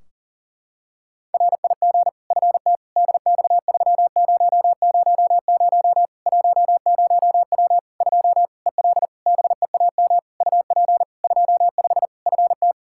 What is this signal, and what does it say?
Morse code, heard as a keyed tone, about 40 wpm, 700 Hz. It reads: RIG FT DX3000 10W 2 EL BEAM UP 25 FT
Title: Practice QSO - Exchange 2 - Info - 40wpm